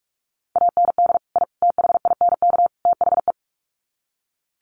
Morse code, heard as a keyed tone, about 44 wpm, 700 Hz. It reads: AND I THINK THE
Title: Sets of 4 Words 45wpm